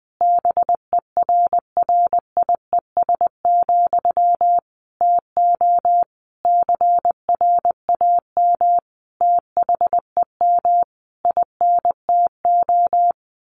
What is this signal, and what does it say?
Morse code, heard as a keyed tone, about 20 wpm, 700 Hz. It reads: BERRIES, TO CRAM THEM INTO
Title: A Sound Of Thunder 20wpm